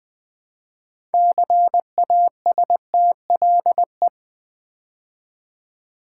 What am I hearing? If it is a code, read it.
Morse code, heard as a keyed tone, about 20 wpm, 700 Hz. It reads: CASTLE
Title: Build A Word - 2 Letters At A Time - 6-Letter Words - 20wpm